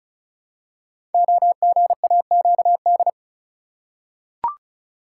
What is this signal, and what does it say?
Morse code, heard as a keyed tone, about 34 wpm, 700 Hz. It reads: OGAQD
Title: Code Groups - 5 Letters - 35wpm